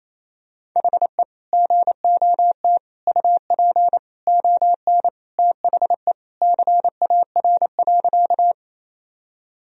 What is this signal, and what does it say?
Morse code, heard as a keyed tone, about 28 wpm, 700 Hz. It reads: HE GOT UP ON THE CAR.
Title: Sentences from Top 1000 Words - Review Part 1 - 28wpm